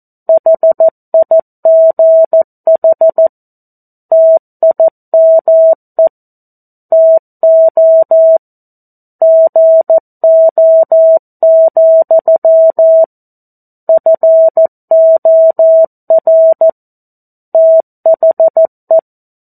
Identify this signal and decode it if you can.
Morse code, heard as a keyed tone, about 14 wpm, 650 Hz. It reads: HIGH TIME TO GO, FOR THE
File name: 02-Chapter-02 14 WPM